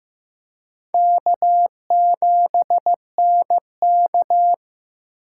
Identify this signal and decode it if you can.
Morse code, heard as a keyed tone, about 16 wpm, 700 Hz. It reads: K7NK
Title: USA Call Signs - 4 Characters - 6x Character Spacing - 15wpm